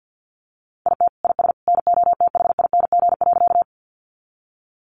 Morse code, heard as a keyed tone, about 50 wpm, 700 Hz. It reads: IT IS NOTHING.
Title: Sentences from Top 500 Words 50wpm